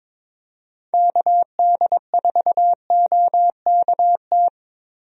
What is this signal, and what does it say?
Morse code, heard as a keyed tone, about 21 wpm, 700 Hz. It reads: KD4OKT
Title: USA Call Signs - 6 Characters - Repeated 3x - 22wpm